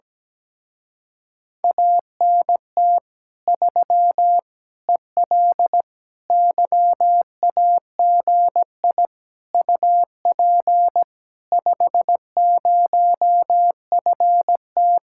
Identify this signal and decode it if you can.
Morse code, heard as a keyed tone, about 17 wpm, 700 Hz. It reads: ANT 3 EL YAGI UP 50FT
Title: Speed-Racing Common QSO Phrases - 17wpm